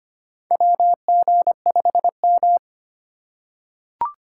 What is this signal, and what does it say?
Morse code, heard as a keyed tone, about 25 wpm, 700 Hz. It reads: W G 5 M
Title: USA Call Signs - 4 Characters - 8x Character Spacing - 25wpm